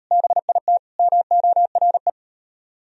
Morse code, heard as a keyed tone, about 41 wpm, 700 Hz. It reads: BIT MORE